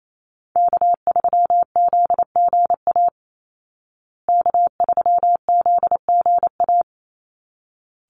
Morse code, heard as a keyed tone, about 28 wpm, 700 Hz. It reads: K3ZGA K3ZGA
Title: Call Signs Repeated 3x - 28wpm